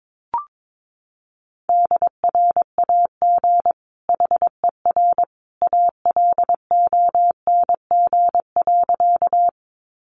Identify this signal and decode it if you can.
Morse code, heard as a keyed tone, about 22 wpm, 700 Hz. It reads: DRAG HER ALONG.
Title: Sentences from Top 1500 Words 22wpm